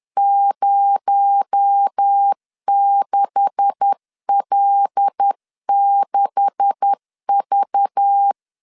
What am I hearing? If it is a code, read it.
Morse code, heard as a keyed tone, about 10 wpm, 800 Hz. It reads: 06L6V
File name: LEZIONI / 9 286 10wpm 2735